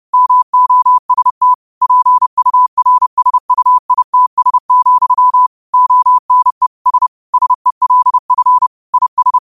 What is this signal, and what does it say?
Morse code, heard as a keyed tone, about 30 wpm, 1000 Hz. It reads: MOST PURSUITS, ONE S SELF IS
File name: qotdfiles / 30wpm